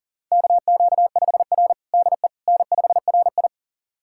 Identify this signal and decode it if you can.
Morse code, heard as a keyed tone, about 40 wpm, 700 Hz. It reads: KQ5R DE N5RI
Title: USA Callsign DE Callsign - 4 Characters - 40wpm